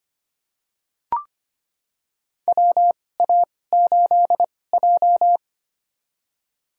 Morse code, heard as a keyed tone, about 25 wpm, spaced as for 18 wpm, 700 Hz. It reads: WA8J
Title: USA Call Signs - 4 Characters - 2x Character Spacing - 25wpm